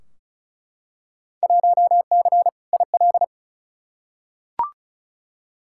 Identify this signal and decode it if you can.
Morse code, heard as a keyed tone, about 36 wpm, 700 Hz. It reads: 1C IL
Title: Field Day Exchanges - 35wpm